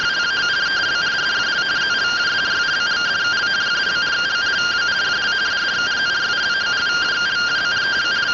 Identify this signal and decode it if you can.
Two-tone teleprinter signal, about 170 Hz shift.